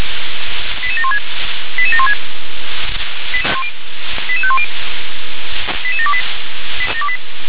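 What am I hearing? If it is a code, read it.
Five-tone selective call (ZVEI), heard as quick runs of five tones, tones of about 70 ms. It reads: 88716 88716 88510 88728
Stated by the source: Five-tone ZVEI selective call received by SDR.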